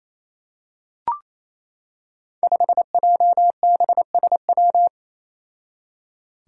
Morse code, heard as a keyed tone, about 27 wpm, 700 Hz. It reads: 5JBSW
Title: Five Challenging Letter-Number Combinations 28wpm